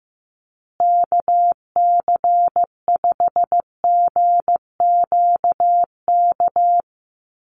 Morse code, heard as a keyed tone, about 15 wpm, 700 Hz. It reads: KC5GQK
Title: USA Call Signs Repeated 2x 15wpm